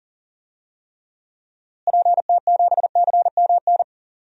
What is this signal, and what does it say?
Morse code, heard as a keyed tone, about 39 wpm, 700 Hz. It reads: PT7CMN
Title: DXCC-100 Call Signs - 6 Characters - Repeated 2x - 40wpm